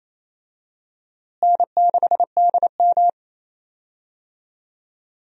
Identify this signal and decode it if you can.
Morse code, heard as a keyed tone, about 28 wpm, 700 Hz. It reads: N6DM
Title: USA Call Signs - 4 Characters - Repeated 3x - 28wpm